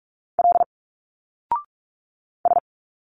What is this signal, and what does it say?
Morse code, heard as a keyed tone, about 46 wpm, 700 Hz. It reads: L S
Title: Single Letters 45wpm